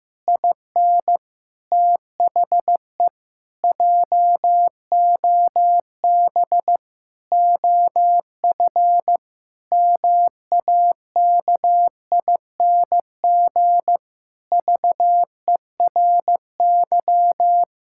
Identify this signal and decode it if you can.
Morse code, heard as a keyed tone, about 15 wpm, 700 Hz. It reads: IN THE JOB OF MAKING VERY
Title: Pangram Challenge v2 - 15wpm